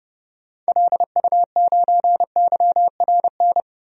Morse code, heard as a keyed tone, about 30 wpm, 700 Hz. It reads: LU9YRN
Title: DXCC-100 Call Signs - 6 Characters - 30wpm